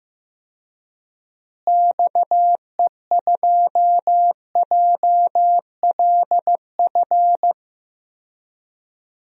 Morse code, heard as a keyed tone, about 15 wpm, 700 Hz. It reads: XE2JLF
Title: DXCC-100 Call Signs - 6 Characters - Repeated 3x - 15wpm